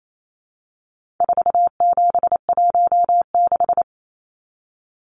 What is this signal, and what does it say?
Morse code, heard as a keyed tone, about 28 wpm, 700 Hz. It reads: 4716
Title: Numbers - 4-Digit 28wpm